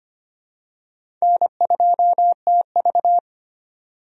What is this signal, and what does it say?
Morse code, heard as a keyed tone, about 25 wpm, 700 Hz. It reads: N2TV
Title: USA Call Signs - 4 Characters - 25wpm